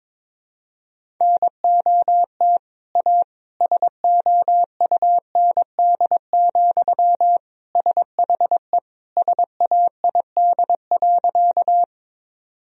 Morse code, heard as a keyed tone, about 22 wpm, 700 Hz. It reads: NOT A SOUND, SHE SAID.
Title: Sentences from Top 1000 Words - Review Part 1 - 22wpm